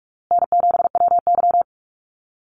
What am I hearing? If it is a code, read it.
Morse code, heard as a keyed tone, about 44 wpm, 700 Hz. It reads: N7WY
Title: Call Signs Repeated 3x - 45wpm